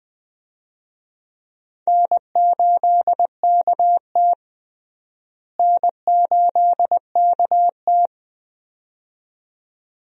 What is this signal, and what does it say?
Morse code, heard as a keyed tone, about 20 wpm, 700 Hz. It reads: N8KT N8KT
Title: USA Call Signs - 4 Characters - Repeated 2x - 20wpm